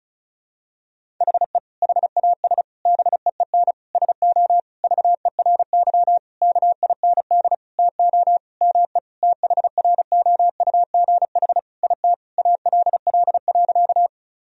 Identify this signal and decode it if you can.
Morse code, heard as a keyed tone, about 35 wpm, 700 Hz. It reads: HE HAS BEEN SO VERY KIND TO ME THROUGH IT ALL.
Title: Sentences from Top 1000 Words - Review Part 1 - 35wpm